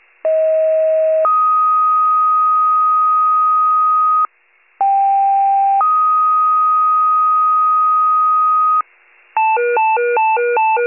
Two-tone page, heard as a one-second tone, then a three-second tone, about 640 then 1250 Hz, about 790 then 1250 Hz. It is heard from 0.2 to 8.8 s.
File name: pager / 38 FIRE / wcalert